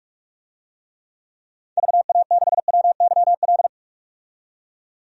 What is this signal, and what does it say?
Morse code, heard as a keyed tone, about 43 wpm, 700 Hz. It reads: VA6WYL
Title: DXCC-100 Call Signs - 6 Characters - 45wpm